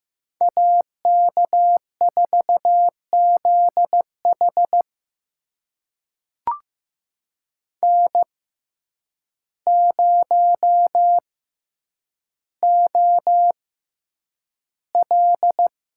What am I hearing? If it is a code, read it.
Morse code, heard as a keyed tone, about 15 wpm, spaced as for 8 wpm, 700 Hz. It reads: AK4ZH N0OL
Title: USA Call Signs - 5 Characters - 6x Character Spacing - 15wpm